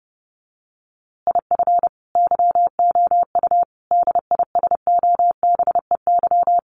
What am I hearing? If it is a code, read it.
Morse code, heard as a keyed tone, about 30 wpm, 700 Hz. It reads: IF YOU DISOBEY
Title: A Sound Of Thunder 30wpm